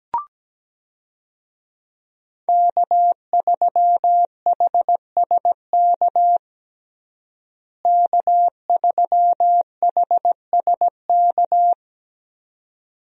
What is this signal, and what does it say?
Morse code, heard as a keyed tone, about 17 wpm, 700 Hz. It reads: K3HSK K3HSK
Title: USA Call Signs - 5 Characters - Repeated 2x - 17wpm